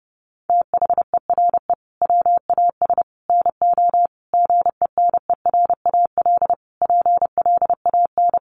Morse code, heard as a keyed tone, about 30 wpm, 700 Hz. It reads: THERE WAS NO GENERAL PLAN
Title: Sentences from Top 700 Words 30wpm